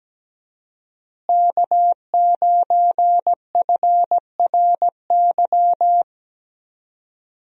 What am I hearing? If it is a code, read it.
Morse code, heard as a keyed tone, about 17 wpm, 700 Hz. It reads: K9FRY
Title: USA Call Signs - 5 Characters - Repeated 3x - 17wpm